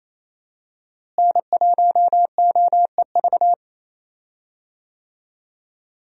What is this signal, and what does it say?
Morse code, heard as a keyed tone, about 28 wpm, 700 Hz. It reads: N1OEV
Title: USA Call Signs - 5 Characters - Repeated 3x - 28wpm